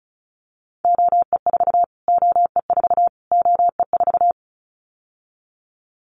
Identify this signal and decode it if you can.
Morse code, heard as a keyed tone, about 34 wpm, 700 Hz. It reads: OE4 OE4 OE4
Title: Callsign Prefixes for Top 25 DXCC - Repeated 3x - 35wpm